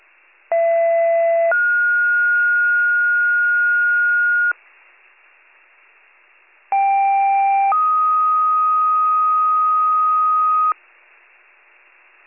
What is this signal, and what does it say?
Two-tone page, heard as a one-second tone, then a three-second tone, about 670 then 1430 Hz, about 790 then 1250 Hz. It is heard from 0.5 to 10.7 s.